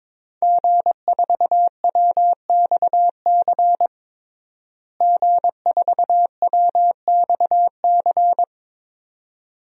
Morse code, heard as a keyed tone, about 22 wpm, 700 Hz. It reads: G4WXC G4WXC
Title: DXCC-100 Call Signs - 5 Characters - Repeated 2x - 22wpm